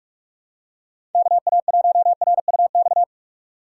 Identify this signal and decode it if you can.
Morse code, heard as a keyed tone, about 44 wpm, 700 Hz. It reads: KA1RUX
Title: USA Call Signs - 6 Characters - 4x Character Spacing - 45wpm